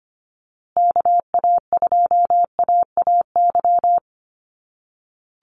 Morse code, heard as a keyed tone, about 25 wpm, 700 Hz. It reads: KA2AAY
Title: USA Call Signs Repeated 3x 25wpm